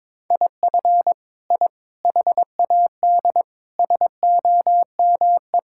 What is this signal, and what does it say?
Morse code, heard as a keyed tone, about 22 wpm, 700 Hz. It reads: IF I HAD SOME
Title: Sentences from Top 1000 Words - Review Part 5 - 22wpm